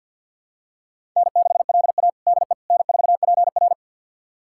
Morse code, heard as a keyed tone, about 52 wpm, 700 Hz. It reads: N6LA DE N4PR
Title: USA Callsign DE Callsign - 4 Characters - 50wpm